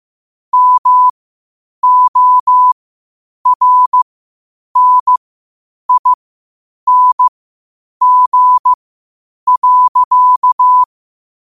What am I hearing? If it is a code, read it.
Morse code, heard as a keyed tone, about 15 wpm, spaced as for 8 wpm, 1000 Hz. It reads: MORNING.
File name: qotdfiles / 8wpm